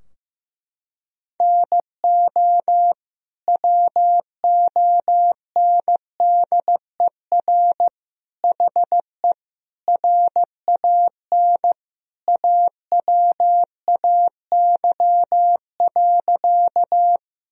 Morse code, heard as a keyed tone, about 15 wpm, 700 Hz. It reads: NO WONDER HE RAN AWAY.
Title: Sentences from Top 1000 Words - Review Part 3 - 15wpm